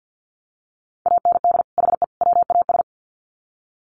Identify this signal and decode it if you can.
Morse code, heard as a keyed tone, about 50 wpm, 700 Hz. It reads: AND HE WAS
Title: Sets of 3 Words 50wpm